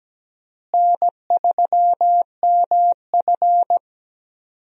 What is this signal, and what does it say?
Morse code, heard as a keyed tone, about 17 wpm, 700 Hz. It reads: N3MF
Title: USA Call Signs - 4 Characters - Repeated 3x - 17wpm